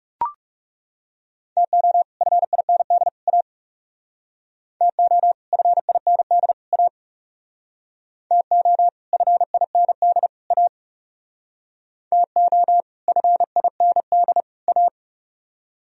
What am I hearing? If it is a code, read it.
Morse code, heard as a keyed tone, about 37 wpm, 700 Hz. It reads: TO FIND A TO FIND A TO FIND A TO FIND A
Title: Speed-Racing - Sets of 3 Words 30wpm